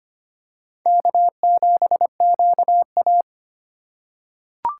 Morse code, heard as a keyed tone, about 25 wpm, 700 Hz. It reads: K7QA
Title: USA Call Signs - 4 Characters - 4x Character Spacing - 25wpm